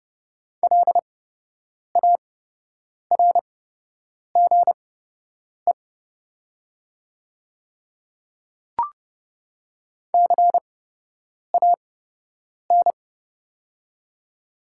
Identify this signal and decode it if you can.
Morse code, heard as a keyed tone, about 30 wpm, spaced as for 8 wpm, 690 Hz. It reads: LARGE CAN
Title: 8x Character Spacing - Top 200 Words - 30wpm